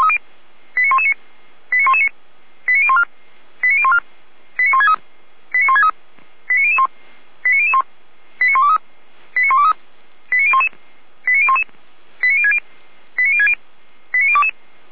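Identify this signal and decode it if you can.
Five-tone selective call (ZVEI), heard as quick runs of five tones, tones of about 70 ms. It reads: 89119 89119 89014 89014 89163 89163 89001 89001 89123 89123 89011 89011 89060 89060 89022
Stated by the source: ZVEI five-tone selective call received by SDR, from the Feuerwehr Orlamünde (fire brigade) audio server.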